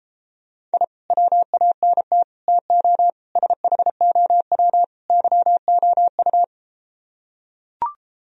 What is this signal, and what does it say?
Morse code, heard as a keyed tone, about 33 wpm, 700 Hz. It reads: I WANT TO SHOW YOU